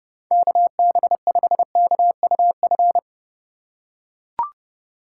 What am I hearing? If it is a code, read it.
Morse code, heard as a keyed tone, about 32 wpm, 700 Hz. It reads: KB5KUF